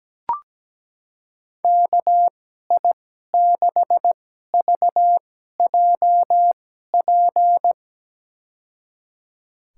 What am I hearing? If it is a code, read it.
Morse code, heard as a keyed tone, about 17 wpm, spaced as for 12 wpm, 700 Hz. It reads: KI6VJP
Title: USA Call Signs - 6 Characters - 2x Character Spacing - 17wpm